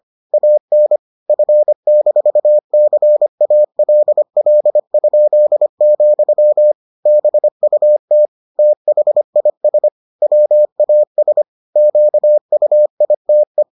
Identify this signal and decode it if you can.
Morse code, heard as a keyed tone, about 25 wpm, 600 Hz. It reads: AN F-CALL?, BUT THIS WAS QUITE